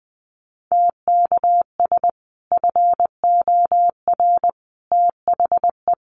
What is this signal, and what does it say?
Morse code, heard as a keyed tone, about 19 wpm, 700 Hz. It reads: TKS FOR THE
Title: Common QSO Phrases 20wpm